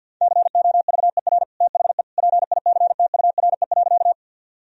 Morse code, heard as a keyed tone, about 50 wpm, 700 Hz. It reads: COVER THE PICTURE.
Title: Sentences from Top 1000 Words - Review Part 1 - 50wpm